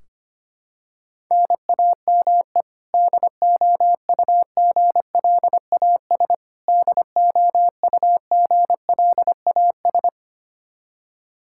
Morse code, heard as a keyed tone, about 25 wpm, 700 Hz. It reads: NAME DOUGLAS DOUGLAS
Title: Practice QSO - Exchange 1 - Beginning - 25wpm